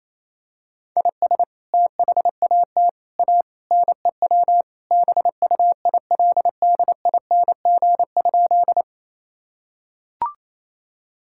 Morse code, heard as a keyed tone, about 28 wpm, 700 Hz. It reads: IS THAT A NEW BUILDING?
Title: Sentences from Top 1000 Words - Review Part 5 - 28wpm